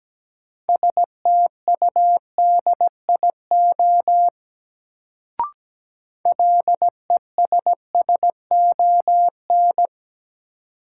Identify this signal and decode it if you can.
Morse code, heard as a keyed tone, about 17 wpm, 700 Hz. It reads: STUDIO LESSON
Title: Sentences from Top 1200 Words - New Word Drill - 17wpm